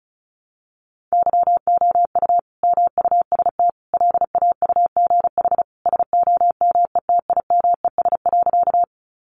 Morse code, heard as a keyed tone, about 34 wpm, 700 Hz. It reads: YOU MUST LAUGH SOMETIMES.
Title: Sentences from Top 600 Words 35wpm